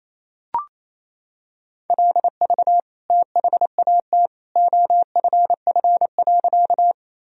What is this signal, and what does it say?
Morse code, heard as a keyed tone, about 28 wpm, 700 Hz. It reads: LV THAT OFF.
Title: Sentences from Top 400 Words and Abbreviations 28wpm